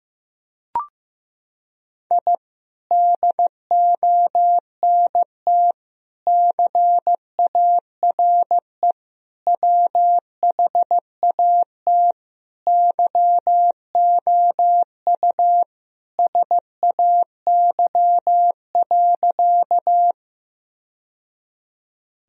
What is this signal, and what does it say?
Morse code, heard as a keyed tone, about 15 wpm, 700 Hz. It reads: I DONT CARE WHAT YOU SAY.
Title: Sentences from Top 1000 Words - Review Part 1 - 15wpm